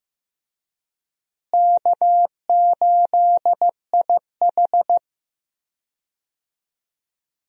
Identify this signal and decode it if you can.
Morse code, heard as a keyed tone, about 15 wpm, 700 Hz. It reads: K8IH
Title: USA Call Signs - 4 Characters - Repeated 2x - 15wpm